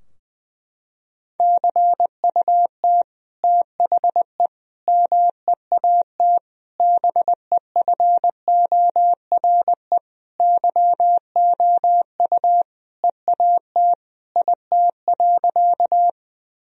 Morse code, heard as a keyed tone, about 20 wpm, 700 Hz. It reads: CUT THE MEAT BEFORE YOU EAT IT.
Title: Sentences from Top 1000 Words - Review Part 3 - 20wpm